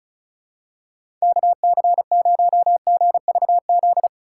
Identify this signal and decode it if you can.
Morse code, heard as a keyed tone, about 36 wpm, 700 Hz. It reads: KC0GVZ
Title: USA Call Signs - 6 Characters - 3x Character Spacing - 35wpm